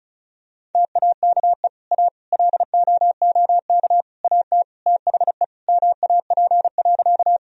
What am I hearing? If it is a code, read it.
Morse code, heard as a keyed tone, about 34 wpm, 700 Hz. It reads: TAKE A LOOK AT THE MAP.
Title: Sentences from Top 1000 Words - Review Part 2 - 35wpm